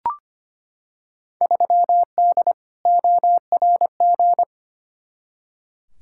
Morse code, heard as a keyed tone, about 25 wpm, 700 Hz. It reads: 3D ORG
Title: Field Day Exchanges Encore - 25wpm